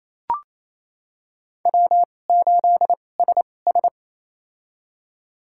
Morse code, heard as a keyed tone, about 28 wpm, spaced as for 20 wpm, 700 Hz. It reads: W8SS
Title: USA Call Signs - 4 Characters - 2x Character Spacing - 28wpm